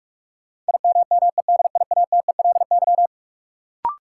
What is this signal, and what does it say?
Morse code, heard as a keyed tone, about 46 wpm, 700 Hz. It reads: IMMEDIATELY
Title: Sentences from Top 1100 Words - New Word Drill - 45wpm